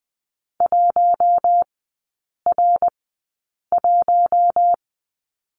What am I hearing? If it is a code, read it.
Morse code, heard as a keyed tone, about 20 wpm, spaced as for 8 wpm, 700 Hz. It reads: 1R1
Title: 053 Lesson 10e - Letter r - Three Characters 20wpm